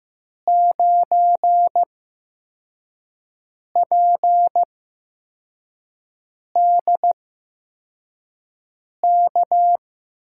Morse code, heard as a keyed tone, about 15 wpm, spaced as for 4 wpm, 700 Hz. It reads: 9PDK
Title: USA Call Signs - 5 Characters - 8x Character Spacing - 15wpm